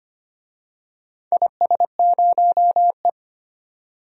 Morse code, heard as a keyed tone, about 25 wpm, 700 Hz. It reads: IS0E
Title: DXCC-100 Call Signs - 4 Characters - Repeated 2x - 25wpm